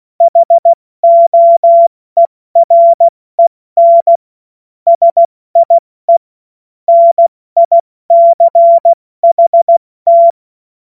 Morse code, heard as a keyed tone, about 16 wpm, spaced as for 14 wpm, 680 Hz. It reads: HOEREN SIE NICHT
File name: Morsekurs / Witze / Witz 26. Oktober 2021, 70 F80